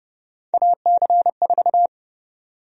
Morse code, heard as a keyed tone, about 31 wpm, 700 Hz. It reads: AC4
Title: USA Call Sign Prefixes - Repeated 2x - 30wpm